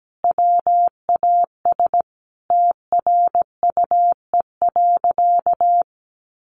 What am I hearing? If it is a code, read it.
Morse code, heard as a keyed tone, about 17 wpm, 700 Hz. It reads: WAS TRUE.
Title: Sentences from Top 400 Words 17wpm